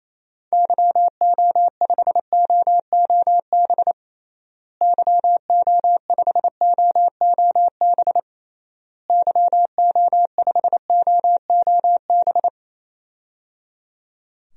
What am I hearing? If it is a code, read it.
Morse code, heard as a keyed tone, about 27 wpm, 700 Hz. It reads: YO5OOB YO5OOB YO5OOB
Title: DXCC-100 Call Signs - 6 Characters - Repeated 3x - 28wpm